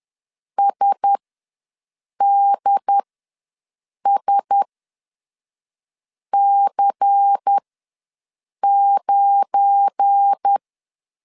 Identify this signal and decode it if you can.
Morse code, heard as a keyed tone, about 10 wpm, spaced as for 5 wpm, 800 Hz. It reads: SDSC9
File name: LEZIONI / 7 GF9Z / 7 GF9Z ws 10wpm 3919